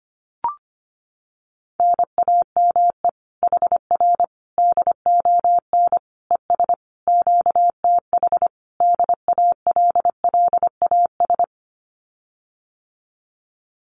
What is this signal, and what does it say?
Morse code, heard as a keyed tone, about 25 wpm, 700 Hz. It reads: NAME HR DON ES QTH DALLAS
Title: Common QSO Phrases Encore 25wpm